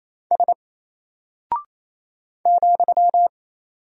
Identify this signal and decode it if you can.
Morse code, heard as a keyed tone, about 28 wpm, 700 Hz. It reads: S ,
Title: Single Character (Letter:Number:Punctuation:Prosign) Alt 28wpm